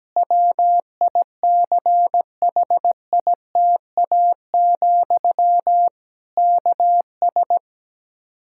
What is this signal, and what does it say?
Morse code, heard as a keyed tone, about 17 wpm, 700 Hz. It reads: WICHITA, KS
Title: Practice QSO - Exchange 1 - Beginning - 17wpm